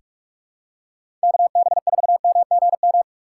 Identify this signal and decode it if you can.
Morse code, heard as a keyed tone, about 43 wpm, 700 Hz. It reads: KB4MGM
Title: USA Call Signs - 6 Characters - 2x Character Spacing - 45wpm